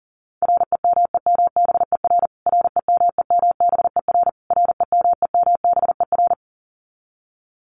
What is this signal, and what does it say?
Morse code, heard as a keyed tone, about 40 wpm, 700 Hz. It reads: REMEMBER REMEMBER REMEMBER
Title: Top 500 Words Repeated 40wpm